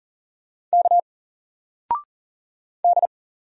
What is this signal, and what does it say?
Morse code, heard as a keyed tone, about 40 wpm, 700 Hz. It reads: K D
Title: Single Letter-Number 40wpm